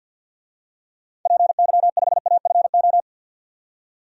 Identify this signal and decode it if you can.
Morse code, heard as a keyed tone, about 47 wpm, 700 Hz. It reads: PY5AFO
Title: DXCC-100 Call Signs - 6 Characters - 50wpm